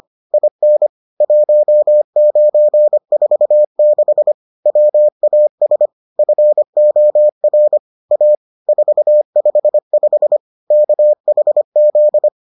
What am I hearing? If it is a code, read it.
Morse code, heard as a keyed tone, about 25 wpm, 600 Hz. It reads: IN 1946 WAS FOR A 455 KHZ